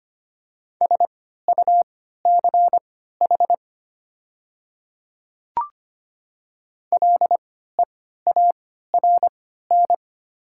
Morse code, heard as a keyed tone, about 25 wpm, spaced as for 14 wpm, 700 Hz. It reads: SUCH LEARN
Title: 3x Character Spacing - Top 300 Words - 25wpm